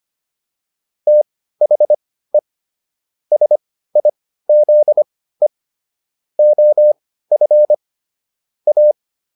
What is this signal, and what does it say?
Morse code, heard as a keyed tone, about 25 wpm, spaced as for 15 wpm, 600 Hz. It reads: THE SIZE OF A